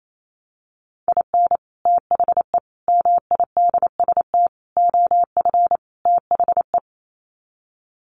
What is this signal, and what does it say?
Morse code, heard as a keyed tone, about 28 wpm, 700 Hz. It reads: IN THE MIDST OF THE
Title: Sets of 5 Words 28wpm